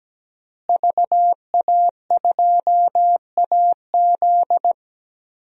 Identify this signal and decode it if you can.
Morse code, heard as a keyed tone, about 17 wpm, 700 Hz. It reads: VA2AZ
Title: DXCC-100 Call Signs - 5 Characters - Repeated 3x - 17wpm